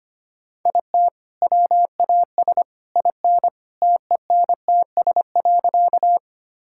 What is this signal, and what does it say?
Morse code, heard as a keyed tone, about 25 wpm, 700 Hz. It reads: IT WAS IN TENTS.
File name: practice / 25wpm / All-Age Appropriate Jokes - 25wpm